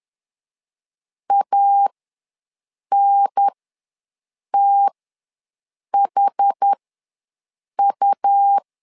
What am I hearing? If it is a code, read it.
Morse code, heard as a keyed tone, about 10 wpm, spaced as for 6 wpm, 800 Hz. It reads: ANTHU
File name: 2 UHDN ws 10wpm